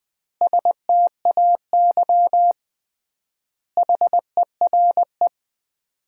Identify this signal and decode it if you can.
Morse code, heard as a keyed tone, about 20 wpm, 700 Hz. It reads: STAY HERE
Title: 3x Word-Spacing - Sentences from Top 500 Words - 20wpm